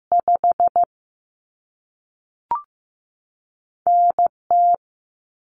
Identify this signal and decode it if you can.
Morse code, heard as a keyed tone, about 15 wpm, 700 Hz. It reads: 5 NT
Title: Common Cut Numbers - 2-Digit 15wpm